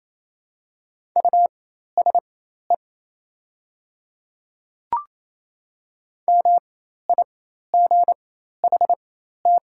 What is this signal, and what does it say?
Morse code, heard as a keyed tone, about 27 wpm, spaced as for 13 wpm, 690 Hz. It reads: USE MIGHT
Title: 4x Character Spacing - Top 300 Words - 28wpm